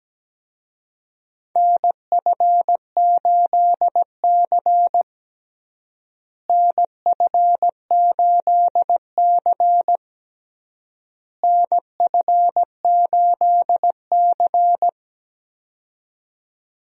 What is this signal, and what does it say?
Morse code, heard as a keyed tone, about 17 wpm, 700 Hz. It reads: NF8C NF8C NF8C
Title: USA Call Signs - 4 Characters - Repeated 3x - 17wpm